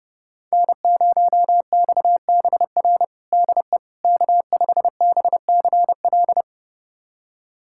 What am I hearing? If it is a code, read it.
Morse code, heard as a keyed tone, about 32 wpm, 700 Hz. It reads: N0XBR DE K5BCL
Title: USA Callsign DE Callsign - 5 Characters - 30wpm